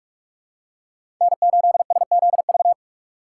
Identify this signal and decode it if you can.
Morse code, heard as a keyed tone, about 46 wpm, 700 Hz. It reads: N8SZV
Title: USA Call Signs - 5 Characters - 4x Character Spacing - 45wpm